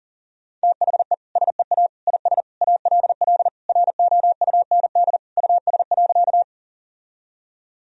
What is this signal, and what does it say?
Morse code, heard as a keyed tone, about 40 wpm, 700 Hz. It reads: THE SEA IS ALL ROUND US.
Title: Sentences from Top 1000 Words - Review Part 1 - 40wpm